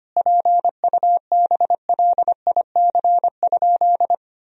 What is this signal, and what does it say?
Morse code, heard as a keyed tone, about 25 wpm, 700 Hz. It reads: PUBLIC?
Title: Sentences from Top 1000 Words - Review Part 3 - 25wpm